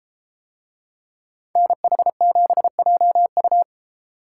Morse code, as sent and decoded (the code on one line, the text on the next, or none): -. .... --... .--- ..-
NH7JU